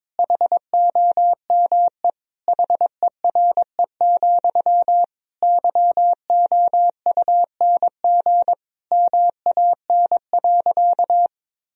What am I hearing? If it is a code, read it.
Morse code, heard as a keyed tone, about 22 wpm, 700 Hz. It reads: HOME HERE, YOUNG MAN.
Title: Sentences from Top 1000 Words - Review Part 1 - 22wpm